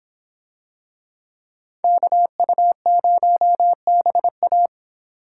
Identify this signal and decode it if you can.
Morse code, heard as a keyed tone, about 26 wpm, 700 Hz. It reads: KU0BA